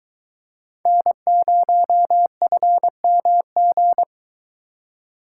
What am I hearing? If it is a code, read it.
Morse code, heard as a keyed tone, about 21 wpm, 700 Hz. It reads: N0FMG